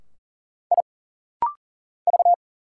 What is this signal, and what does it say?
Morse code, heard as a keyed tone, about 40 wpm, 700 Hz. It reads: I V
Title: Single Letter-Number - Rapid-Fire 40wpm